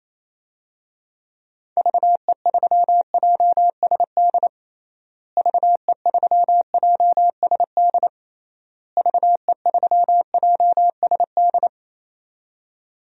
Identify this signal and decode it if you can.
Morse code, heard as a keyed tone, about 28 wpm, 700 Hz. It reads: VE3JSD VE3JSD VE3JSD
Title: DXCC-100 Call Signs - 6 Characters - Repeated 3x - 28wpm